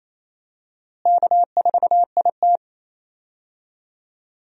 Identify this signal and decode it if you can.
Morse code, heard as a keyed tone, about 28 wpm, 700 Hz. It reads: K4IT
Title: USA Call Signs - 4 Characters - Repeated 3x - 28wpm